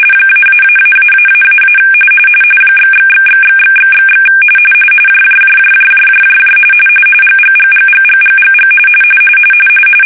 RTTY, about 45.45 baud. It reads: --------- ///////// JJJJJJJJJ GGGGGGGGG
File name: fsk-ascii850-pix / tiddly winks